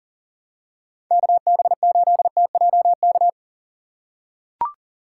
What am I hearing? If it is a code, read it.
Morse code, heard as a keyed tone, about 40 wpm, 700 Hz. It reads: KB8TJK
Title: USA Call Signs - 6 Characters - 4x Character Spacing - 40wpm